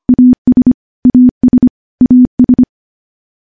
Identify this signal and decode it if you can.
Morse code, heard as a keyed tone, about 25 wpm, 250 Hz. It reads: AS AS AS
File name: morse-training-25WPM